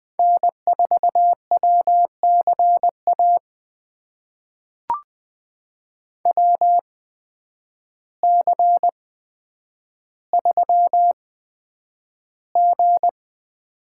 Morse code, heard as a keyed tone, about 20 wpm, spaced as for 9 wpm, 700 Hz. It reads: N4WCA WC3G
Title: USA Call Signs - 8x Character Spacing - 20wpm